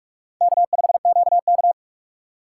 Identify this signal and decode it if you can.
Morse code, heard as a keyed tone, about 46 wpm, 700 Hz. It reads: K5QK
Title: USA Call Signs - 4 Characters - 2x Character Spacing - 45wpm